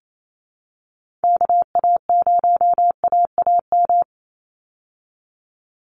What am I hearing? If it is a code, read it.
Morse code, heard as a keyed tone, about 28 wpm, 700 Hz. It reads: KA0AAM
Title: Call Signs Repeated 3x - 28wpm